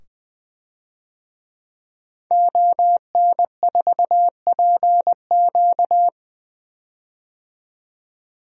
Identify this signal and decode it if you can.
Morse code, heard as a keyed tone, about 20 wpm, 700 Hz. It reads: ON4PQ
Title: DXCC-100 Call Signs - 5 Characters - 20wpm